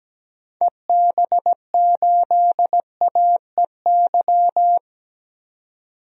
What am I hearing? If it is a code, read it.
Morse code, heard as a keyed tone, about 17 wpm, 700 Hz. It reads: EB8AEY
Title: DXCC-100 Call Signs - 6 Characters - 17wpm